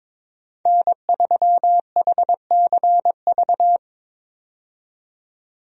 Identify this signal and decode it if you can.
Morse code, heard as a keyed tone, about 23 wpm, 700 Hz. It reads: N3HCV
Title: USA Call Signs - 5 Characters - 22wpm